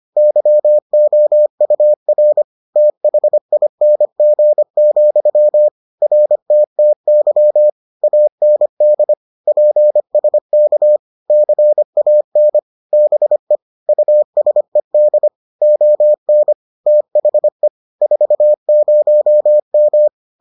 Morse code, heard as a keyed tone, about 25 wpm, 600 Hz. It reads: YOUR THING, RTTY AND PSK CAN BE USED ON THE 40M